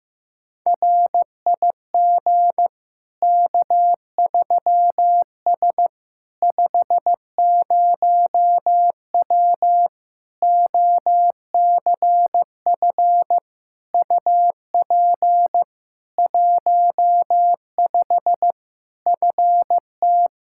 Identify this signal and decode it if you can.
Morse code, heard as a keyed tone, about 15 wpm, 700 Hz. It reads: RIG K3S 50W OCF UP 15 FT
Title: K1USN SST QSO Practice 15wpm